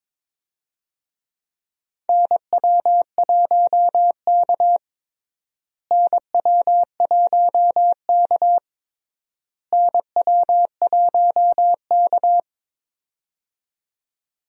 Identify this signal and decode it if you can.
Morse code, heard as a keyed tone, about 22 wpm, 700 Hz. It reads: NW1K NW1K NW1K
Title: USA Call Signs - 4 Characters - Repeated 3x - 22wpm